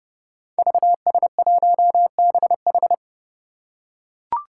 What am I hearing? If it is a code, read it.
Morse code, heard as a keyed tone, about 30 wpm, 700 Hz. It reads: VS1BH
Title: Five Challenging Letter-Number Combinations 30wpm